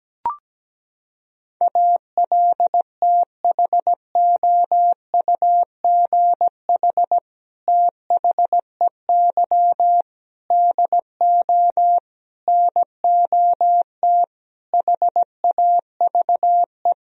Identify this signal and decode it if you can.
Morse code, heard as a keyed tone, about 17 wpm, 700 Hz. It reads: ALTHOUGH THEY DO NOT HAVE
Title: Sentences from Top 1000 Words - Review Part 3 - 17wpm